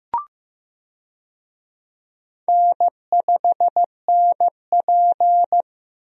Morse code, heard as a keyed tone, about 15 wpm, 700 Hz. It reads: N5NP
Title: USA Call Signs - 4 Characters - Repeated 3x - 15wpm